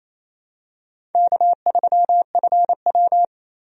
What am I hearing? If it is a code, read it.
Morse code, heard as a keyed tone, about 28 wpm, 700 Hz. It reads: K3FW
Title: USA Call Signs - 4 Characters - 28wpm